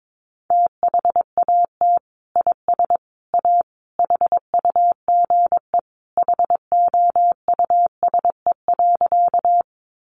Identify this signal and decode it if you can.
Morse code, heard as a keyed tone, about 22 wpm, 700 Hz. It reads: THAT IS A HUGE HOUSE.
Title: Sentences from Top 1100 Words 22wpm